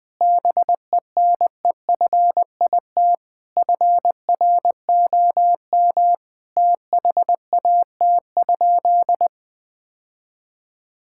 Morse code, heard as a keyed tone, about 21 wpm, 700 Hz. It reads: BENEFIT FROM THAT?
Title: Sentences from Top 1000 Words - Review Part 6 - 20wpm